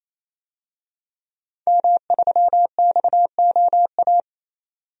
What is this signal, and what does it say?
Morse code, heard as a keyed tone, about 28 wpm, 700 Hz. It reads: M3XOA
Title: DXCC-100 Call Signs - 5 Characters - 28wpm